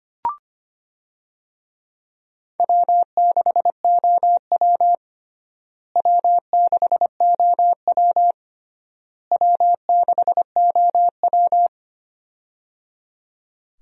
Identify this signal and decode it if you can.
Morse code, heard as a keyed tone, about 25 wpm, 700 Hz. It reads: W6OW W6OW W6OW
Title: USA Call Signs - 4 Characters - Repeated 3x - 25wpm